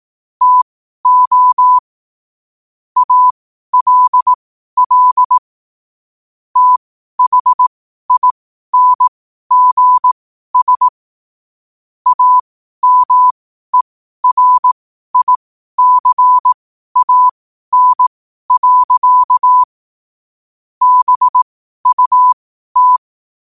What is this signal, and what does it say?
Morse code, heard as a keyed tone, about 18 wpm, spaced as for 12 wpm, 1000 Hz. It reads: TO ALL THINGS AMERICAN. BUT
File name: qotdfiles / 12wpm